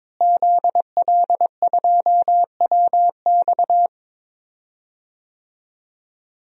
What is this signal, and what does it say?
Morse code, heard as a keyed tone, about 22 wpm, 700 Hz. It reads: ZL2WX
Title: DXCC-100 Call Signs - 5 Characters - Repeated 2x - 22wpm